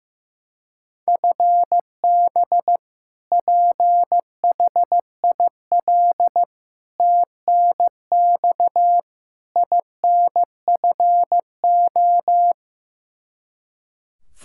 Morse code, heard as a keyed tone, about 15 wpm, 700 Hz. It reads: FB PHIL TNX INFO
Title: Practice QSO - Exchange 3 - More Info - 15wpm